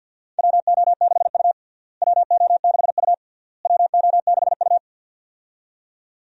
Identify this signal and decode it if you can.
Morse code, heard as a keyed tone, about 53 wpm, 700 Hz. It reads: WO6U WO6U WO6U
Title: USA Call Signs - 4 Characters - Repeated 3x - 50wpm